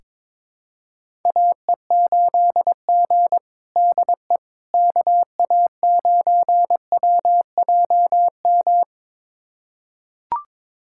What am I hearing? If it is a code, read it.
Morse code, heard as a keyed tone, about 21 wpm, 700 Hz. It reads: AE8G DE KA9WJM
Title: USA Callsign DE Callsign 22wpm